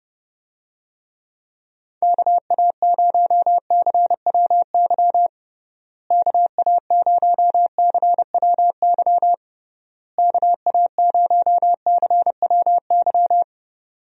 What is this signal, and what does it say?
Morse code, heard as a keyed tone, about 30 wpm, 700 Hz. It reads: KA0CWY KA0CWY KA0CWY
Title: USA Call Signs - 6 Characters - Repeated 3x - 30wpm